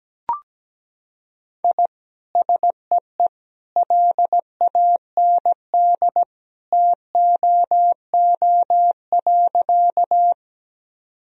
Morse code, heard as a keyed tone, about 17 wpm, 700 Hz. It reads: I SEE LAND TOO.
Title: Sentences from Top 1000 Words - Review Part 1 - 17wpm